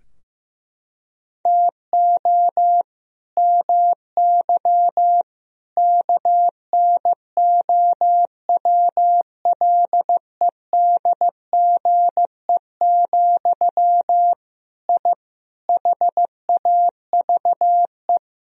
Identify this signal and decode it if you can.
Morse code, heard as a keyed tone, about 15 wpm, 700 Hz. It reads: TO MY KNOWLEDGE, I HAVE
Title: Sentences from Top 1000 Words - Review Part 3 - 15wpm